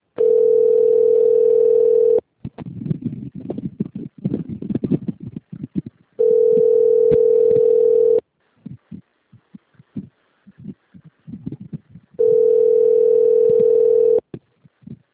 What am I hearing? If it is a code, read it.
Ringback tone (North America), 440 + 480 Hz.
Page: Jaguariuna - SP